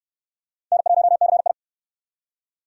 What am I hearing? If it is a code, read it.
Morse code, heard as a keyed tone, about 68 wpm, 700 Hz. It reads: N1CI